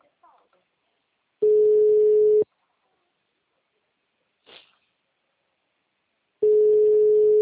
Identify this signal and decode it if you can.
Ringback tone (Europe, 425 Hz), with speech.